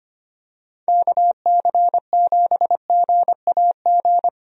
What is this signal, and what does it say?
Morse code, heard as a keyed tone, about 25 wpm, 700 Hz. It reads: KC7GAG
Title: USA Call Signs - 6 Characters - Repeated 2x - 25wpm